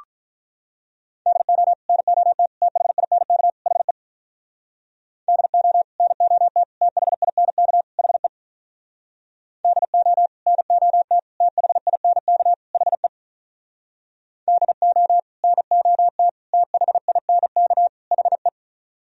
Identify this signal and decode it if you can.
Morse code, heard as a keyed tone, about 47 wpm, 700 Hz. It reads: DO NOT THINK HE DO NOT THINK HE DO NOT THINK HE DO NOT THINK HE